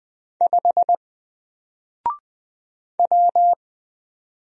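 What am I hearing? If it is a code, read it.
Morse code, heard as a keyed tone, about 20 wpm, 700 Hz. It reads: 5 W
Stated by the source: Single letter and number Morse drill at 20 wpm, from N4LCD.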